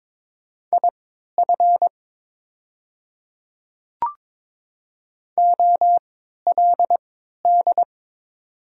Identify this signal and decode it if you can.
Morse code, heard as a keyed tone, about 22 wpm, spaced as for 13 wpm, 690 Hz. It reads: IF OLD
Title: 3x Character Spacing - Top 200 Words - 22wpm